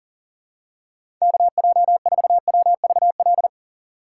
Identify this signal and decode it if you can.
Morse code, heard as a keyed tone, about 40 wpm, 700 Hz. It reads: KJ4WVL
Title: USA Call Signs - 6 Characters - 3x Character Spacing - 40wpm